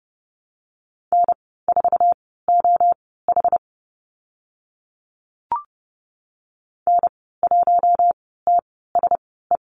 Morse code, heard as a keyed tone, about 30 wpm, spaced as for 17 wpm, 700 Hz. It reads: N4OH N1TSE
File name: Lesson 14g - Number 2 - Call Signs 30wpm